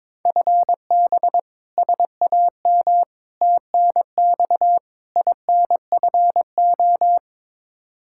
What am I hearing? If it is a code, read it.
Morse code, heard as a keyed tone, about 21 wpm, 700 Hz. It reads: FB SAM TNX INFO
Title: Practice QSO - Exchange 3 - More Info - 22wpm